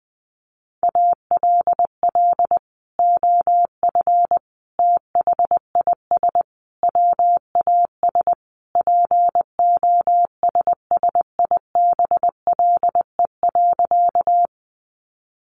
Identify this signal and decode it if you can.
Morse code, heard as a keyed tone, about 20 wpm, 700 Hz. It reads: ALL OF THIS WAS POSSIBLE.
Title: Sentences from Top 500 Words 20wpm